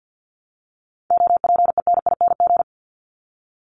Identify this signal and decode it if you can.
Morse code, heard as a keyed tone, about 48 wpm, 700 Hz. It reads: OPENING
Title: Top 1751-2000 Words 50wpm